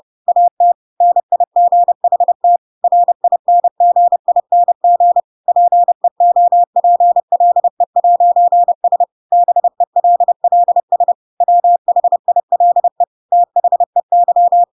Morse code, heard as a keyed tone, about 30 wpm, 700 Hz. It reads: AT NIGHT RINGING PEOPLE'S BELLS WHILE THEY
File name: Pinocchio / Pinocchio ch6 30 30 ik1hge